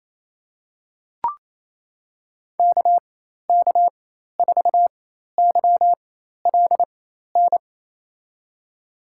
Morse code, heard as a keyed tone, about 28 wpm, spaced as for 13 wpm, 700 Hz. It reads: KK4YLN
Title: USA Call Signs - 6 Characters - 4x Character Spacing - 28wpm